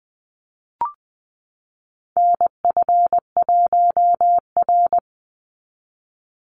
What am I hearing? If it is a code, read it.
Morse code, heard as a keyed tone, about 20 wpm, 700 Hz. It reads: NF1R
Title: NA SOTA Chaser Callsigns 20wpm